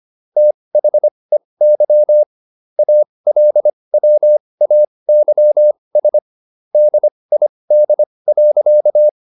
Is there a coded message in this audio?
Yes: Morse code from 0.4 to 9.1 s, heard as a keyed tone, about 25 wpm, spaced as for 20 wpm, 600 Hz. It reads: THEY ALWAYS DID.